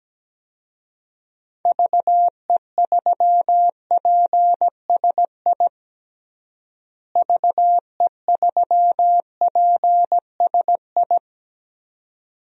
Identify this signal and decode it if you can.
Morse code, heard as a keyed tone, about 17 wpm, 700 Hz. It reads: VE3PSI VE3PSI
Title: DXCC-100 Call Signs - 6 Characters - Repeated 2x - 17wpm